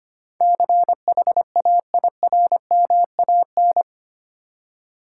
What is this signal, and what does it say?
Morse code, heard as a keyed tone, about 25 wpm, 690 Hz. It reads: CHAIRMAN
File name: 8 Letter Words - First 3 Letters Given - 25wpm